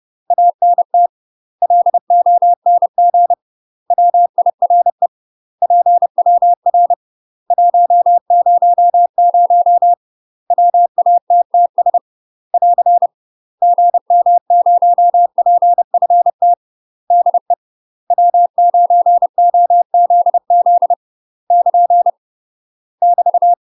Morse code, heard as a keyed tone, about 30 wpm, 700 Hz. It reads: ANT LONG WIRE PWR 100 WATTS <AR> GM0PFT DE W9OZZ <KN> <BT>
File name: UK MASTER QSOs TRAINING / 30WPM / UK MASTER QSOs 0046